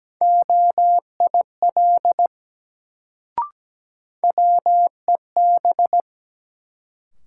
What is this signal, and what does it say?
Morse code, heard as a keyed tone, about 17 wpm, 700 Hz. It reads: OIL WEB
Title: Sentences from Top 1000 Words - Review Part 4 - New Word Drill - 17wpm